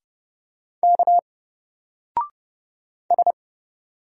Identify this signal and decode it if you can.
Morse code, heard as a keyed tone, about 30 wpm, 700 Hz. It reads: K S
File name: Single-Letter-Number-30wpm